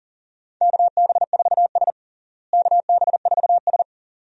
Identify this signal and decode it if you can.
Morse code, heard as a keyed tone, about 40 wpm, 700 Hz. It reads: KB4S KB4S
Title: USA Call Signs - 4 Characters - Repeated 2x - 40wpm